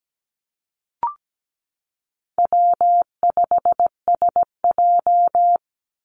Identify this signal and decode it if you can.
Morse code, heard as a keyed tone, about 17 wpm, 700 Hz. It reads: W5SJ
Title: Easy Call Signs Repeated 2x - 17wpm